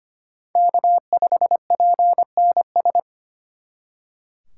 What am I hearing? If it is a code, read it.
Morse code, heard as a keyed tone, about 31 wpm, 700 Hz. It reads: K5PNS
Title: Speed Racing - USA 1x3 Call Signs 25wpm